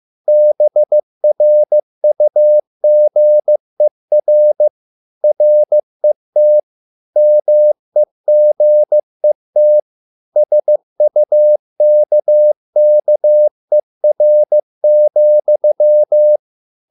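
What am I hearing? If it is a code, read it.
Morse code, heard as a keyed tone, about 15 wpm, 600 Hz. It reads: BRUGER RET MEGET SUKKER,
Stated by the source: Morse code at a slow 15 wpm from CW med Gnister.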